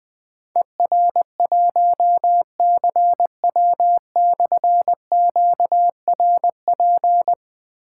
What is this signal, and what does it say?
Morse code, heard as a keyed tone, about 20 wpm, 700 Hz. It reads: ER1CW/QRP
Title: Difficult Call Signs 20wpm